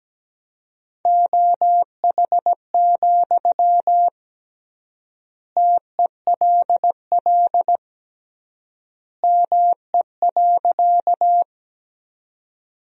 Morse code, heard as a keyed tone, about 17 wpm, 700 Hz. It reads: OH, TELL ME.
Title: 3x Word-Spacing - Sentences from Top 500 Words - 17wpm